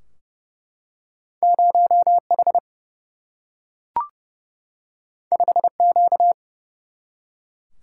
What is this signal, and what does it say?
Morse code, heard as a keyed tone, about 30 wpm, 700 Hz. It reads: 0 H 5 Q
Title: Single-Letter Single-Number 30wpm